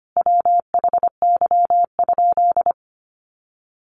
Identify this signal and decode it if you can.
Morse code, heard as a keyed tone, about 25 wpm, 700 Hz. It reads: WHY?
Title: Sentences from Top 200 Words 25wpm